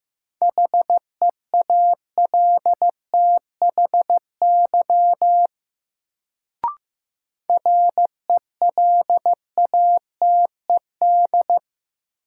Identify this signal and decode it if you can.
Morse code, heard as a keyed tone, about 15 wpm, 700 Hz. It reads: HEALTHY RELATED
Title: Sentences from Top 1000 Words - Review Part 5 - New Word Drill - 15wpm